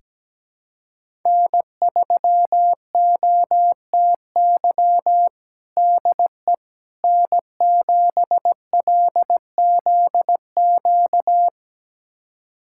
Morse code, heard as a keyed tone, about 17 wpm, 700 Hz. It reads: N3OTY DE N7LZQ
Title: USA Callsign DE Callsign - 5 Characters - 17wpm